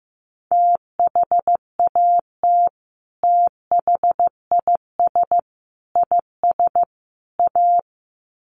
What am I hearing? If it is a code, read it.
Morse code, heard as a keyed tone, about 15 wpm, 700 Hz. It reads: THAT THIS IS A
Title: Sets of 4 Words 15wpm